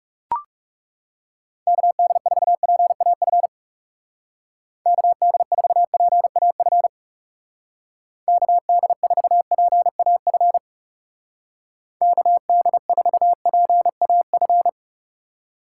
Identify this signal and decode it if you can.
Morse code, heard as a keyed tone, about 38 wpm, 700 Hz. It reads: KD4PAF KD4PAF KD4PAF KD4PAF